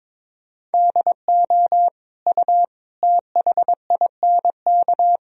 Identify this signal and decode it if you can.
Morse code, heard as a keyed tone, about 22 wpm, 700 Hz. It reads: DO U THINK
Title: Sentences from Top 200 Words and Abbreviations 22wpm